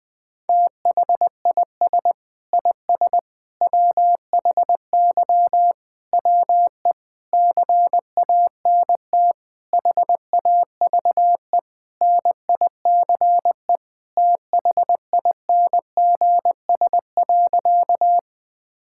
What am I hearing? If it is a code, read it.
Morse code, heard as a keyed tone, about 20 wpm, 700 Hz. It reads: THIS IS WHY WE CANT HAVE NICE THINGS.
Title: Sentences with 8-Word Phrases - Spoken Beginning - 20wpm